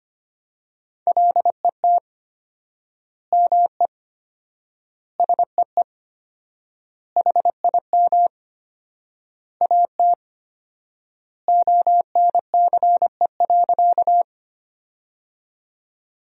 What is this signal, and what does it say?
Morse code, heard as a keyed tone, about 25 wpm, 700 Hz. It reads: LET ME SEE HIM AT ONCE.
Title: 4x Word-Spacing - Sentences from Top 300 Words - 25wpm